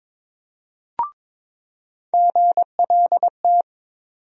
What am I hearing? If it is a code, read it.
Morse code, heard as a keyed tone, about 22 wpm, 700 Hz. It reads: GLT
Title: USA Call Sign Postfixes - 3 Letters 22wpm